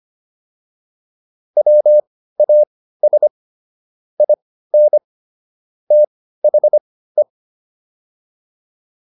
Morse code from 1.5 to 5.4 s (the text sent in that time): WAS IN